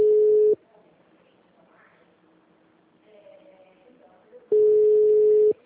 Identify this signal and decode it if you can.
Ringback tone (Europe, 425 Hz), with speech.
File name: out- -2002- - - .4135